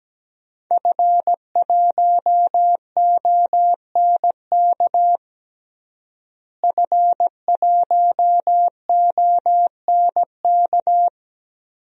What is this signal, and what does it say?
Morse code, heard as a keyed tone, about 17 wpm, 700 Hz. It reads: F1ONK F1ONK
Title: DXCC-100 Call Signs - 5 Characters - Repeated 3x - 17wpm